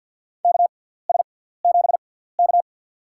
Morse code, heard as a keyed tone, about 48 wpm, spaced as for 17 wpm, 700 Hz. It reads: KS7K
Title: USA Call Signs - 4 Characters - 6x Character Spacing - 50wpm